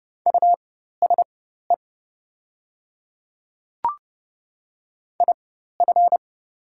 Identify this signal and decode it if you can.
Morse code, heard as a keyed tone, about 30 wpm, spaced as for 14 wpm, 700 Hz. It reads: USE IF
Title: 4x Character Spacing - Top 100 Words - 30wpm